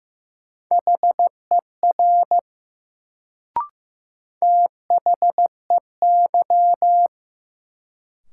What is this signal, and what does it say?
Morse code, heard as a keyed tone, about 15 wpm, 700 Hz. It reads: HER THEY
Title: Sentences from Top 100 Words - New Word Drill - 15wpm